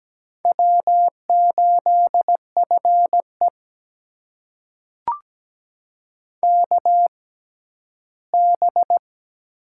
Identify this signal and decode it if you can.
Morse code, heard as a keyed tone, about 17 wpm, spaced as for 11 wpm, 700 Hz. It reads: W8FE KB
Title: USA Call Signs - 6x Character Spacing - 17wpm